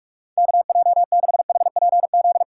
Morse code, heard as a keyed tone, about 45 wpm, 700 Hz. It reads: KJ6HPZ
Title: USA Call Signs - 6 Characters - 4x Character Spacing - 45wpm